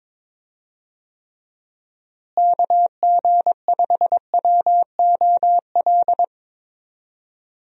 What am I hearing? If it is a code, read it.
Morse code, heard as a keyed tone, about 22 wpm, 700 Hz. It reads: KG5WOL
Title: USA Call Signs - 6 Characters - Repeated 3x - 22wpm